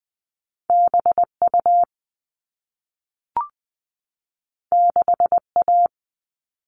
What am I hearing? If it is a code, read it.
Morse code, heard as a keyed tone, about 20 wpm, 700 Hz. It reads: BU 6A
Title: All Cut Numbers - 2-Digit 20wpm